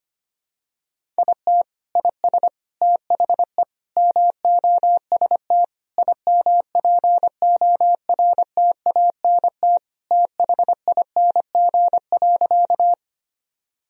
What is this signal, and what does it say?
Morse code, heard as a keyed tone, about 25 wpm, 700 Hz. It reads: IT IS THE MOST IMPORTANT THING.
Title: Sentences from Top 1000 Words - Review Part 1 - 25wpm